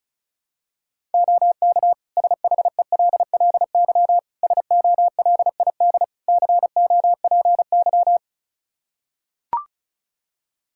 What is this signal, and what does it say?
Morse code, heard as a keyed tone, about 34 wpm, 700 Hz. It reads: OK SHELLY SOLID COPY
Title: Practice QSO - Exchange 3 - More Info - 35wpm